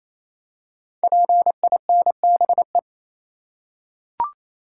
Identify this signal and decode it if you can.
Morse code, heard as a keyed tone, about 28 wpm, 700 Hz. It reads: PINBE
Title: Code Groups - 5 Letter:Numbers - 28wpm